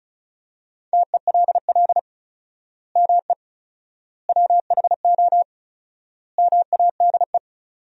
Morse code, heard as a keyed tone, about 34 wpm, 700 Hz. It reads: TELL ME WHO MADE
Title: 4x Word-Spacing - Sentences from Top 200 Words - 35wpm